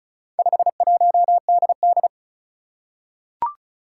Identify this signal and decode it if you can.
Morse code, heard as a keyed tone, about 36 wpm, 700 Hz. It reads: 51DD
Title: Four Challenging Letter-Number Combinations 35wpm